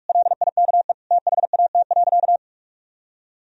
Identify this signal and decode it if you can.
Morse code, heard as a keyed tone, about 46 wpm, 700 Hz. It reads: LIKE THAT.
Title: Sentences from Top 1000 Words - Review Part 3 - 45wpm